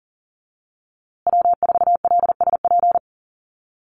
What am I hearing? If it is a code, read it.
Morse code, heard as a keyed tone, about 40 wpm, 700 Hz. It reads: W4LSP
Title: USA Call Signs Repeated 2x 40wpm